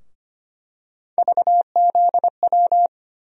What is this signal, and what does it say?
Morse code, heard as a keyed tone, about 25 wpm, 700 Hz. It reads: VZW
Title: USA Call Sign Postfixes - 3 Letters 25wpm